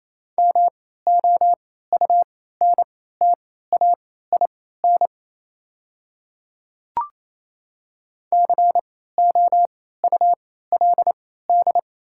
Morse code, heard as a keyed tone, about 28 wpm, spaced as for 16 wpm, 700 Hz. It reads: MOUNTAIN COULD
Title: 3x Character Spacing - Top 300 Words - 28wpm